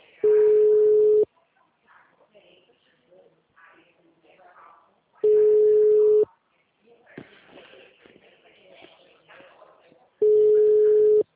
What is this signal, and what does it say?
Ringback tone (Europe, 425 Hz), with speech.
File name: out- -2008- - - .4